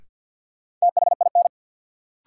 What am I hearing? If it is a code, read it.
Morse code, heard as a keyed tone, about 53 wpm, 700 Hz. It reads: THIN
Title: Sentences from Top 1200 Words - New Word Drill - 50wpm